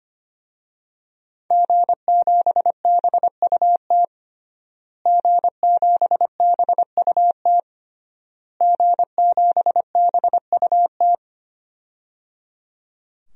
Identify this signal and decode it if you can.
Morse code, heard as a keyed tone, about 25 wpm, 700 Hz. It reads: G7BUT G7BUT G7BUT
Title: DXCC-100 Call Signs - 5 Characters - Repeated 3x - 25wpm